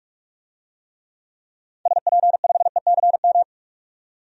Morse code, heard as a keyed tone, about 46 wpm, 700 Hz. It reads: SP5ECM
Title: DXCC-100 Call Signs - 6 Characters - 45wpm